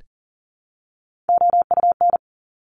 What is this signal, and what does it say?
Morse code, heard as a keyed tone, about 38 wpm, 690 Hz. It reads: OUN
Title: Top Three Letter Combinations in Words 40wpm